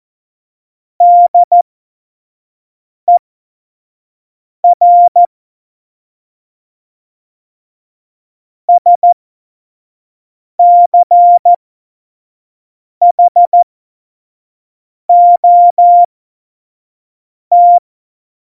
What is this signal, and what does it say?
Morse code, heard as a keyed tone, about 14 wpm, spaced as for 5 wpm, 700 Hz. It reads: DER SCHOT
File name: Morsekurs / Witze / Witz 03. November 2020, 25 F70